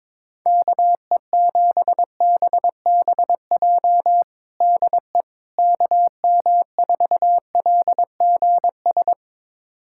Morse code, heard as a keyed tone, about 22 wpm, 700 Hz. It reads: KE7BBJ DE KM4LGS
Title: USA Callsign DE Callsign 22wpm